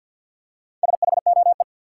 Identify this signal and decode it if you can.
Morse code, heard as a keyed tone, about 52 wpm, 700 Hz. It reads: SHOE
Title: Four Letter Words - 50wpm